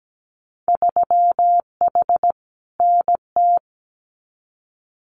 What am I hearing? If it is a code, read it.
Morse code, heard as a keyed tone, about 17 wpm, 700 Hz. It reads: 3H NT
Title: Winter Field Day Exchanges Encore 17wpm